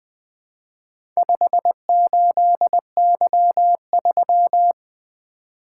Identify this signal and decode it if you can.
Morse code, heard as a keyed tone, about 20 wpm, 700 Hz. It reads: 58Y3
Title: Four Letter-Number Combinations - 20wpm